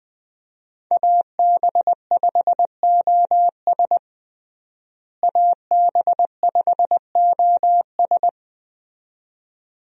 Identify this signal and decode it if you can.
Morse code, heard as a keyed tone, about 20 wpm, 700 Hz. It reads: AB5OS AB5OS
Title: USA Call Signs - 5 Characters - Repeated 2x - 20wpm